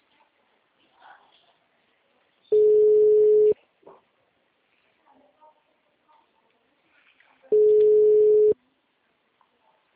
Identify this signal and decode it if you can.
Ringback tone (Europe, 425 Hz), with speech.